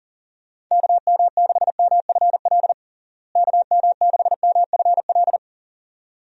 Morse code, heard as a keyed tone, about 37 wpm, 700 Hz. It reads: KM6MFL KM6MFL
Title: USA Call Signs - 6 Characters - Repeated 2x - 40wpm